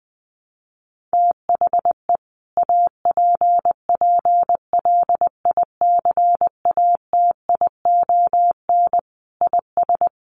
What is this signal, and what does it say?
Morse code, heard as a keyed tone, about 20 wpm, 700 Hz. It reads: THE APPLICATION IS
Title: Sentences from Top 1200 Words 20wpm